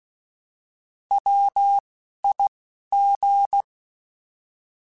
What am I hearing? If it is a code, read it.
Morse code, heard as a keyed tone, about 16 wpm, 800 Hz. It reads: W I G
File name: Alphabetical / Alphabetical mp3s / Ws @10